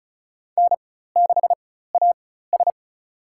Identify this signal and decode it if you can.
Morse code, heard as a keyed tone, about 34 wpm, spaced as for 16 wpm, 700 Hz. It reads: N6AS
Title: USA Call Signs - 4 Characters - 4x Character Spacing - 35wpm